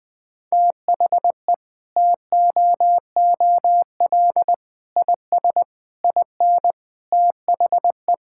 Morse code, heard as a keyed tone, about 20 wpm, 700 Hz. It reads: THE TOOL IS IN THE
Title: Sentences from Top 1000 Words - Review Part 6 - 20wpm